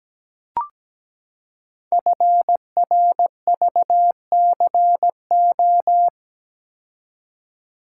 Morse code, heard as a keyed tone, about 17 wpm, 700 Hz. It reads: FRVCO
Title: Code Groups - 5 Letters - 17wpm